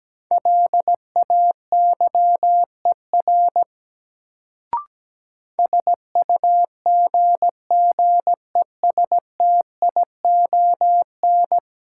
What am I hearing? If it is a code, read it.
Morse code, heard as a keyed tone, about 17 wpm, 700 Hz. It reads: LAYER SUGGESTION
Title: Sentences from Top 1600 Words - New Word Drill Alt - 17wpm